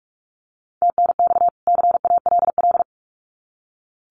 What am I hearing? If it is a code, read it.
Morse code, heard as a keyed tone, about 44 wpm, 700 Hz. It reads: TNX CALL
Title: Common QSO Phrases 45wpm